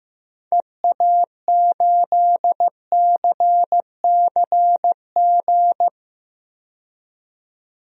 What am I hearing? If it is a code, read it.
Morse code, heard as a keyed tone, about 15 wpm, 700 Hz. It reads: EA8CCG
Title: DXCC-100 Call Signs - 6 Characters - 15wpm